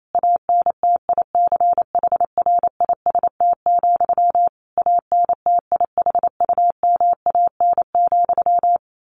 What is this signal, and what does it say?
Morse code, heard as a keyed tone, about 28 wpm, 700 Hz. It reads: ANTICHRIST, ANTIHUMAN,
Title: A Sound Of Thunder 28wpm